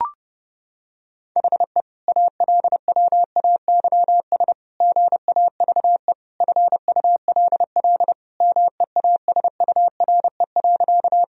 Morse code, heard as a keyed tone, about 30 wpm, 700 Hz. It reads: HE ALWAYS GAVE FULL MEASURE.
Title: Sentences from Top 1000 Words - Review Part 3 - 30wpm